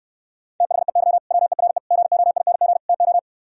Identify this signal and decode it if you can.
Morse code, heard as a keyed tone, about 67 wpm, 700 Hz. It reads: THEY WERE GOING TO